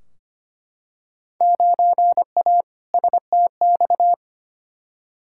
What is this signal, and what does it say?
Morse code, heard as a keyed tone, about 25 wpm, 700 Hz. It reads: 9A STX
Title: Field Day Exchanges - 25wpm